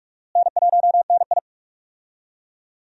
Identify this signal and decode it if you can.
Morse code, heard as a keyed tone, about 46 wpm, 700 Hz. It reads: N1NI
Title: USA Call Signs - 4 Characters - Repeated 2x - 45wpm